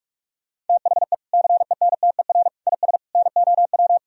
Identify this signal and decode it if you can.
Morse code, heard as a keyed tone, about 46 wpm, 700 Hz. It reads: THE CENTER IS NOW
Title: Sentences from Top 1000 Words - Review Part 2 - 45wpm